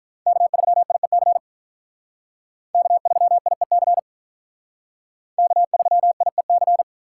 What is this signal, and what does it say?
Morse code, heard as a keyed tone, about 44 wpm, 700 Hz. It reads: K3IEC K3IEC K3IEC